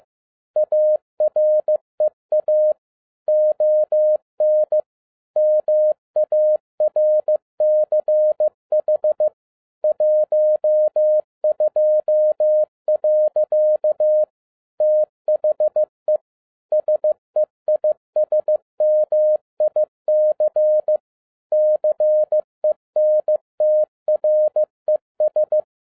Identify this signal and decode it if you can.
Morse code, heard as a keyed tone, about 15 wpm, 610 Hz. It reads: AREA ON MARCH 12. THE SEISMIC CENTRES
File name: 2011-10-mp3 / n02-FIELD DAY CONTEST QRV FROM SAKAE VILLAGE 15wpm